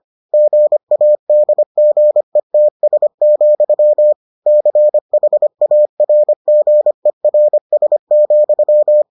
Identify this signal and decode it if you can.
Morse code, heard as a keyed tone, about 25 wpm, 600 Hz. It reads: GADGETS, CHARGERS,